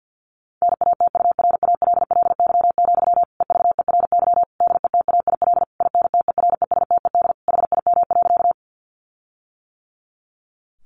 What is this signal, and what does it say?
Morse code, heard as a keyed tone, about 50 wpm, 700 Hz. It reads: NATURALLY, EVERY DETAIL INTERESTED HIM.
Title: Sentences from Top 1500 Words 50wpm